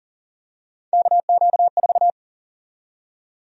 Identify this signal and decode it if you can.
Morse code, heard as a keyed tone, about 40 wpm, 700 Hz. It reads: KQ4
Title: USA Call Sign Prefixes - 40wpm